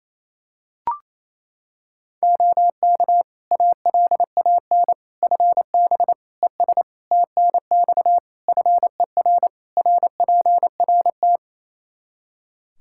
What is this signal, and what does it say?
Morse code, heard as a keyed tone, about 28 wpm, 700 Hz. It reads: OK ALAN FB ES TNX FER RPRT
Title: Practice QSO - Exchange 2 - Info - 28wpm